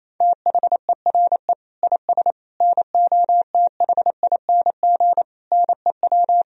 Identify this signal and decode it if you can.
Morse code, heard as a keyed tone, about 28 wpm, 700 Hz. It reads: THERE IS NOTHING NEW
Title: Sentences from Top 1000 Words - Review Part 1 - 28wpm